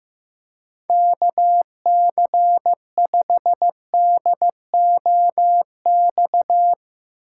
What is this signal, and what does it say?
Morse code, heard as a keyed tone, about 15 wpm, 700 Hz. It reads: KC5DOX
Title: USA Call Signs - 2x Character Spacing - 15wpm